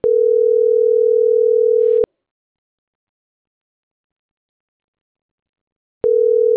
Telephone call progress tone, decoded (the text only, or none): ringback tone